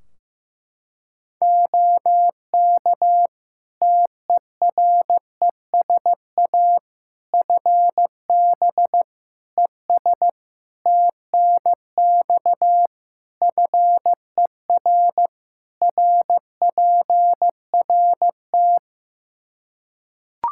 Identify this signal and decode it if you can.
Morse code, heard as a keyed tone, about 15 wpm, 700 Hz. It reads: OK TERESA FB ES TNX FER RPRT
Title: Practice QSO - Exchange 2 - Info - 15wpm